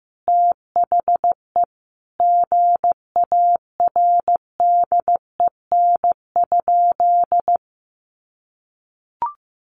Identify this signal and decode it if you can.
Morse code, heard as a keyed tone, about 15 wpm, 700 Hz. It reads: THE GARDEN?
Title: Sentences from Top 700 Words 15wpm